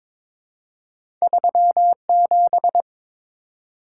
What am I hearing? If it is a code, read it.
Morse code, heard as a keyed tone, about 22 wpm, 700 Hz. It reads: 37
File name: practice / 22wpm / Two Letter-Number Combinations - 22wpm